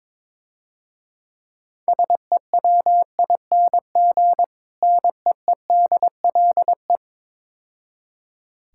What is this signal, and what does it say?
Morse code, heard as a keyed tone, about 22 wpm, 700 Hz. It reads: SEWING NEEDLE
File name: Easy Words - Speed-Racing with Hints - 22wpm